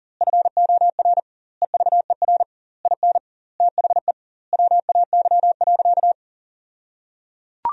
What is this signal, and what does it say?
Morse code, heard as a keyed tone, about 37 wpm, 700 Hz. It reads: FOR EVER IN THE WAY.
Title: 2x Word-Spacing - Sentences from Top 400 Words - 40wpm